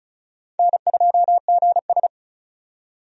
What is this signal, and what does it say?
Morse code, heard as a keyed tone, about 36 wpm, 700 Hz. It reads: N2GS
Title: USA Call Signs - 4 Characters - 35wpm